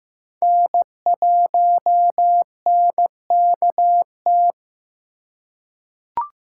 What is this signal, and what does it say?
Morse code, heard as a keyed tone, about 15 wpm, 700 Hz. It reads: N1NKT
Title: USA Call Signs - 3x Character Spacing - 15wpm